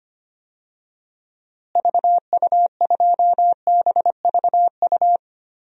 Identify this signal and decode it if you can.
Morse code, heard as a keyed tone, about 25 wpm, 700 Hz. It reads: VU2BVU
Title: DXCC-100 Call Signs - 6 Characters - Repeated 3x - 25wpm